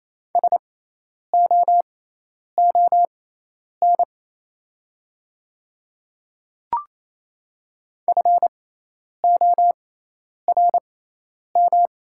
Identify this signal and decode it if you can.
Morse code, heard as a keyed tone, about 28 wpm, spaced as for 10 wpm, 690 Hz. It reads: SOON FORM
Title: 6x Character Spacing - Top 500 Words - 28wpm